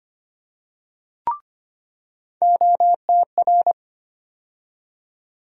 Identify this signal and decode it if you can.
Morse code, heard as a keyed tone, about 25 wpm, 700 Hz. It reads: OTR
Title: USA Call Sign Postfixes - 3 Letters 25wpm